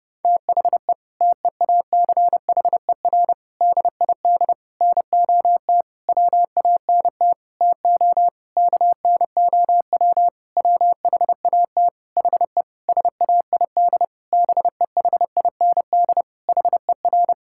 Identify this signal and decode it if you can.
Morse code, heard as a keyed tone, about 30 wpm, 700 Hz. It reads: THE TEACHER DID NOT WANT TO KNOW WHAT HE SAID BEHIND HER
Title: Sentences from Top 1000 Words - Review Part 6 - 30wpm